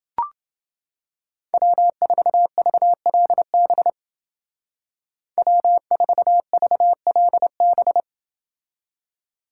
Morse code, heard as a keyed tone, about 28 wpm, 700 Hz. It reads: W4VLB W4VLB